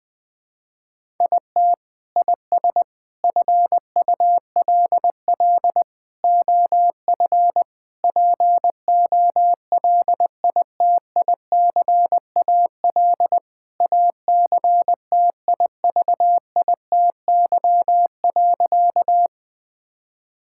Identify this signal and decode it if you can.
Morse code, heard as a keyed tone, about 19 wpm, 700 Hz. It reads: IT IS FULL OF POLITICAL ACTIVITY.
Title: Sentences from Top 1000 Words - Review Part 5 - 20wpm